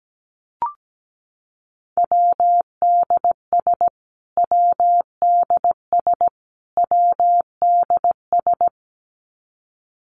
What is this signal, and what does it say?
Morse code, heard as a keyed tone, about 17 wpm, 700 Hz. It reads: WDS WDS WDS
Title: Abbreviations Repeated 3x - 17wpm